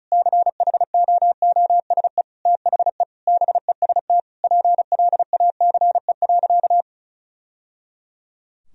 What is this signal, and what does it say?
Morse code, heard as a keyed tone, about 36 wpm, 700 Hz. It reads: CHOOSE THE BEST PLACE.
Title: Sentences from Top 1000 Words - Review Part 4 - 35wpm